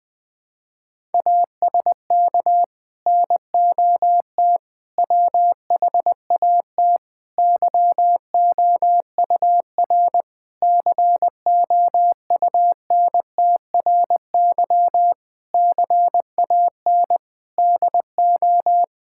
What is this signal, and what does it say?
Morse code, heard as a keyed tone, about 20 wpm, 700 Hz. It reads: ASK NOT WHAT YOUR COUNTRY CAN DO
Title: Sets of 8 Words - 20wpm